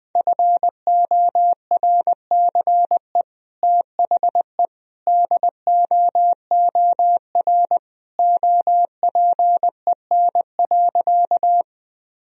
Morse code, heard as a keyed tone, about 21 wpm, 700 Hz. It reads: FORCE THE DOOR OPEN.
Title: Sentences from Top 1000 Words - Review Part 3 - 20wpm